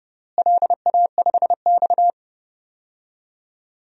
Morse code, heard as a keyed tone, about 32 wpm, 700 Hz. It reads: LA5X
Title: DXCC-100 Call Signs - 4 Characters - Repeated 3x - 30wpm